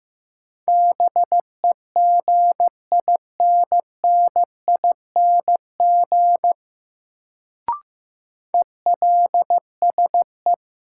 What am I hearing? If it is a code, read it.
Morse code, heard as a keyed tone, about 15 wpm, 700 Hz. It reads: BEGINNING ELSE
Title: Sentences from Top 800 Words - New Word Drill - 15wpm